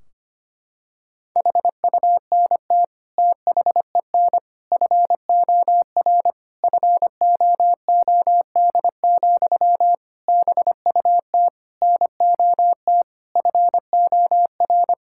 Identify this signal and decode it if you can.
Morse code, heard as a keyed tone, about 25 wpm, 700 Hz. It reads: HUNT THEN FOR FOOD, BUT NOT FOR
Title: Sentences from Top 1600 Words 25wpm